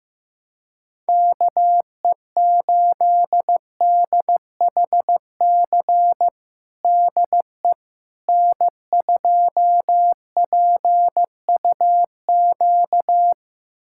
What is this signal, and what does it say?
Morse code, heard as a keyed tone, about 15 wpm, 700 Hz. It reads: KE8DHC DE N2PUQ
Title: USA Callsign DE Callsign 15wpm